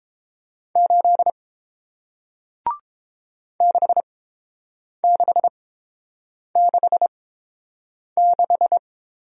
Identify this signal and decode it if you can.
Morse code, heard as a keyed tone, about 30 wpm, spaced as for 7 wpm, 700 Hz. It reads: 8 6666